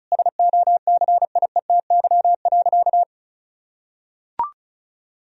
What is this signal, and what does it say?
Morse code, heard as a keyed tone, about 36 wpm, 700 Hz. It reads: SOCIETY.
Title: Sentences from Top 1000 Words - Review Part 4 - 35wpm